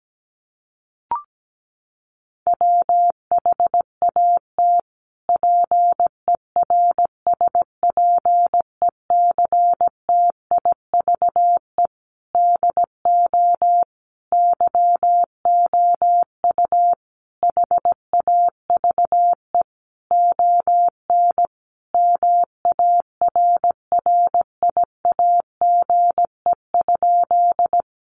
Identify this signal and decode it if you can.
Morse code, heard as a keyed tone, about 17 wpm, 700 Hz. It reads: WHAT PERSPECTIVE DO YOU HAVE ON MARRIAGE?
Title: Sentences from Top 1200 Words 17wpm